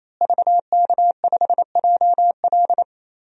Morse code, heard as a keyed tone, about 28 wpm, 700 Hz. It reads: VK5JL
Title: DXCC-100 Call Signs - 5 Characters - Repeated 3x - 28wpm